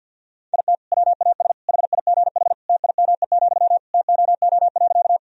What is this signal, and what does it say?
Morse code, heard as a keyed tone, about 52 wpm, 700 Hz. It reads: IT WAS HIGH TIME, TOO.
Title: Sentences from Top 1000 Words - Review Part 1 - 50wpm